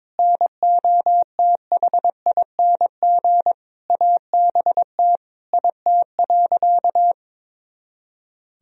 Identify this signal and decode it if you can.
Morse code, heard as a keyed tone, about 22 wpm, 700 Hz. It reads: NOTHING ABT IT.
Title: Sentences from Top 500 Words and Abbreviations 22wpm